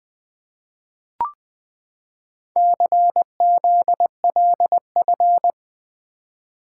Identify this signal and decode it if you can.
Morse code, heard as a keyed tone, about 20 wpm, 700 Hz. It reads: CZLF
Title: Four Letter-Number Combinations - 20wpm